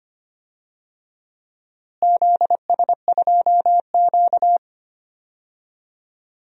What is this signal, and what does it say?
Morse code, heard as a keyed tone, about 25 wpm, 700 Hz. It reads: ZS2Q
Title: DXCC-100 Call Signs - 4 Characters - Repeated 3x - 25wpm